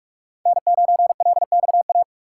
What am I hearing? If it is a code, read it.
Morse code, heard as a keyed tone, about 46 wpm, 700 Hz. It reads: N9LXA
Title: USA Call Signs - 5 Characters - 6x Character Spacing - 45wpm